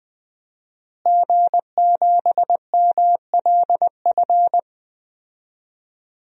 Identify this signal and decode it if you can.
Morse code, heard as a keyed tone, about 20 wpm, 700 Hz. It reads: G7MLF
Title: DXCC-100 Call Signs - 5 Characters - Repeated 3x - 20wpm